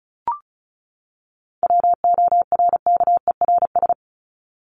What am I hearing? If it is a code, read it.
Morse code, heard as a keyed tone, about 34 wpm, 700 Hz. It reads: WORKERS
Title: ES:S Suffix - Top 100 Words with Base Words 35wpm